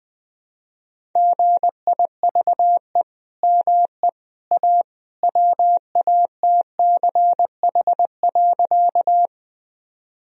Morse code, heard as a keyed tone, about 20 wpm, 700 Hz. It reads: GIVE ME A WATCH.
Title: Sentences from Top 1000 Words - Review Part 1 - 20wpm